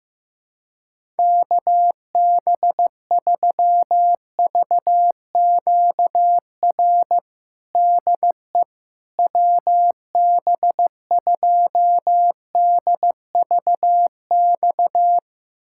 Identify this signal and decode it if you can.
Morse code, heard as a keyed tone, about 15 wpm, 700 Hz. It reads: KB3VQR DE WB2DVX
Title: USA Callsign DE Callsign - 6 Characters - 15wpm